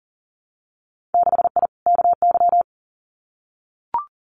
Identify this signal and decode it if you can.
Morse code, heard as a keyed tone, about 39 wpm, 700 Hz. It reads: 6I KY
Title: Winter Field Day Exchanges 40wpm